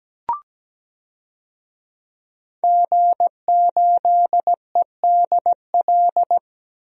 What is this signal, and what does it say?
Morse code, heard as a keyed tone, about 17 wpm, 700 Hz. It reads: G8EDL
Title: DXCC-100 Call Signs - 5 Characters - Repeated 2x - 17wpm